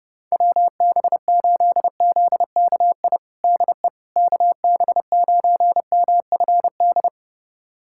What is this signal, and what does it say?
Morse code, heard as a keyed tone, about 32 wpm, 700 Hz. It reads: WB8ZKI DE KB9MFD
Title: USA Callsign DE Callsign - 6 Characters - 30wpm